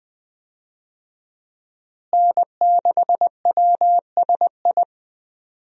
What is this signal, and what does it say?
Morse code, heard as a keyed tone, about 20 wpm, 700 Hz. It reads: N6WSI
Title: USA Call Signs - 5 Characters - Repeated 2x - 20wpm